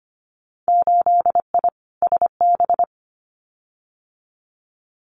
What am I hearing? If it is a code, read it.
Morse code, heard as a keyed tone, about 25 wpm, 700 Hz. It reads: 8I SB
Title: Winter Field Day Exchanges Encore 25wpm